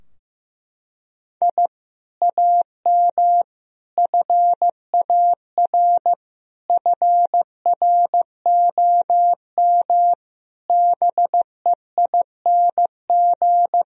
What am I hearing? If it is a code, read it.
Morse code, heard as a keyed tone, about 15 wpm, 700 Hz. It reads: I AM FAR FROM BEING
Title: Sentences from Top 1000 Words - Review Part 3 - 15wpm